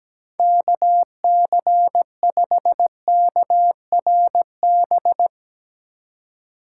Morse code, heard as a keyed tone, about 17 wpm, 700 Hz. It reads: KC5KRB
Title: USA Call Signs - 6 Characters - 6x Character Spacing - 17wpm